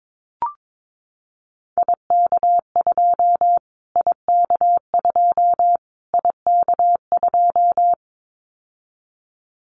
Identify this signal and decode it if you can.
Morse code, heard as a keyed tone, about 22 wpm, 700 Hz. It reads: IK2 IK2 IK2
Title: Callsign Prefixes for Top 25 DXCC - Repeated 3x - 22wpm